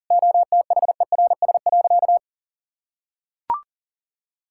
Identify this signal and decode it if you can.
Morse code, heard as a keyed tone, about 40 wpm, 700 Hz. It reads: OTHERS.
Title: Sentences from Top 1000 Words - Review Part 6 - 40wpm